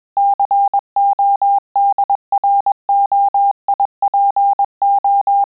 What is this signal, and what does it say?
Morse code, heard as a keyed tone, about 21 wpm, 800 Hz. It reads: CODROIPO
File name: QSO CW 100 CPM